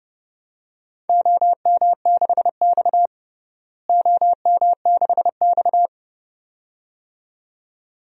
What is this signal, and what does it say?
Morse code, heard as a keyed tone, about 32 wpm, 700 Hz. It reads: OM6X OM6X
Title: DXCC-100 Call Signs - 4 Characters - Repeated 2x - 30wpm